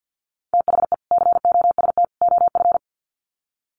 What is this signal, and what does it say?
Morse code, heard as a keyed tone, about 49 wpm, 700 Hz. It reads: THE COST OF
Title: Sets of 3 Words 50wpm